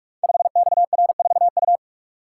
Morse code, heard as a keyed tone, about 44 wpm, 700 Hz. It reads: 5XR4U
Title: Code Groups - 5 Letter:Numbers - 45wpm